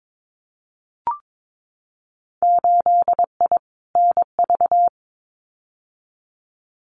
Morse code, heard as a keyed tone, about 22 wpm, 700 Hz. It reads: 8I NV
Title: Winter Field Day Exchanges Encore 22wpm